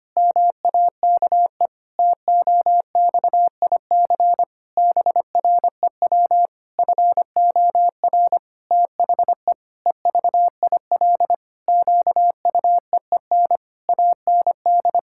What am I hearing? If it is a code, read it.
Morse code, heard as a keyed tone, about 25 wpm, 700 Hz. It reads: MAKE TOXIC BREW FOR THE EVIL QUEEN AND
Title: Pangram Challenge v2 - 25wpm